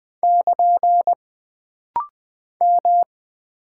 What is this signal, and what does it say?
Morse code, heard as a keyed tone, about 20 wpm, 700 Hz. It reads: <KN> M
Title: Single Character (Letter:Number:Punctuation:Prosign) - Rapid-Fire - 20wpm